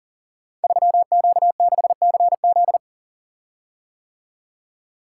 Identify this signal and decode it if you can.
Morse code, heard as a keyed tone, about 41 wpm, 700 Hz. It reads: 3Q6CZ
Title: Code Groups - 5 Letter:Numbers - 40wpm